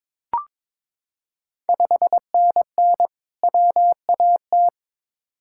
Morse code, heard as a keyed tone, about 22 wpm, 700 Hz. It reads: 5NN WAT
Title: Ontario QSO Party Exchange 22wpm